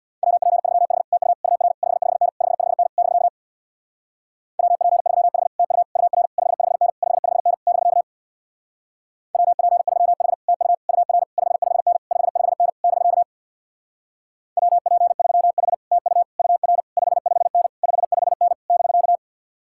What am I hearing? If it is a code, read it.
Morse code, heard as a keyed tone, about 64 wpm, 700 Hz. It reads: WP3H TU UR 55N 55N <BK> WP3H TU UR 55N 55N <BK> WP3H TU UR 55N 55N <BK> WP3H TU UR 55N 55N <BK>